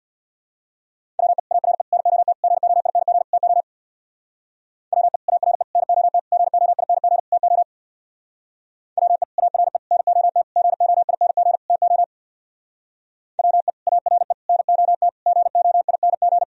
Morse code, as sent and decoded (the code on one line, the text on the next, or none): .-- . / .- .-. / -. --- - / --. --- .. -. --. / - --- / .-- . / .- .-. / -. --- - / --. --- .. -. --. / - --- / .-- . / .- .-. / -. --- - / --. --- .. -. --. / - --- / .-- . / .- .-. / -. --- - / --. --- .. -. --.
WE ARE NOT GOING TO WE ARE NOT GOING TO WE ARE NOT GOING TO WE ARE NOT GOING